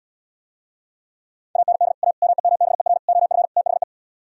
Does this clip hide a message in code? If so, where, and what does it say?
Yes: Morse code from 1.6 to 3.8 s, heard as a keyed tone, about 77 wpm, 700 Hz. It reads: AND A MEMBER OF THE